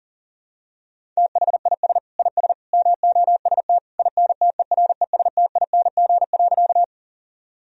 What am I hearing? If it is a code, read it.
Morse code, heard as a keyed tone, about 40 wpm, 700 Hz. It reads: THIS IS MOST INTERESTING.
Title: Sentences from Top 1000 Words - Review Part 1 - 40wpm